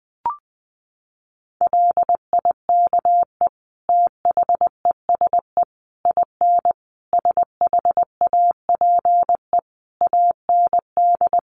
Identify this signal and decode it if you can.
Morse code, heard as a keyed tone, about 20 wpm, 700 Hz. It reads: LIKE THESE IN SHAPE AND
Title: Sentences from Top 500 Words 20wpm